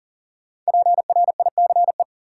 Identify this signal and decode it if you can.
Morse code, heard as a keyed tone, about 38 wpm, 700 Hz. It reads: PRICE
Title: Sentences from Top 600 Words - New Word Drill - 40wpm